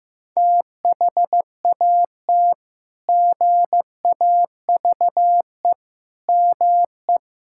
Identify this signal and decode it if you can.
Morse code, heard as a keyed tone, about 15 wpm, 700 Hz. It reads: THAT GAVE ME
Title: Sentences from Top 1600 Words 15wpm